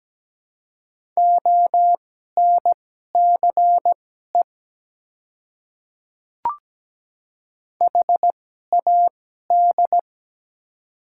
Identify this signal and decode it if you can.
Morse code, heard as a keyed tone, about 17 wpm, spaced as for 12 wpm, 700 Hz. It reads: ONCE HAD
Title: 2x Character Spacing - Top 300 Words - 17wpm